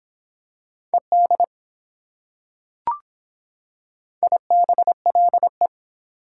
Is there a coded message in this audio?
Yes: Morse code from 0.9 to 5.7 s, heard as a keyed tone, about 24 wpm, 700 Hz. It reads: ED IBLE